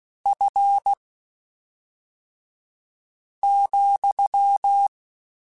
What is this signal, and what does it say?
Morse code, heard as a keyed tone, about 16 wpm, spaced as for 3 wpm, 800 Hz. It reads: F,
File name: Spaced Characters / Spaced Characters-mp3s / Weather4 24 sp @5